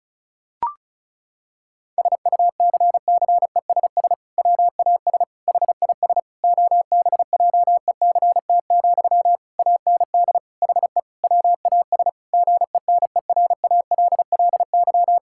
Morse code, heard as a keyed tone, about 34 wpm, 690 Hz. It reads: SUCCESS WAS HIS OBJECT, AND HE WAS GENERALLY
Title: Sentences from Top 1000 Words - Review Part 6 - 35wpm